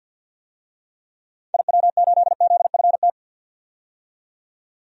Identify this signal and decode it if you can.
Morse code, heard as a keyed tone, about 50 wpm, 700 Hz. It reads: IW8ZFT
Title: DXCC-100 Call Signs - 6 Characters - 50wpm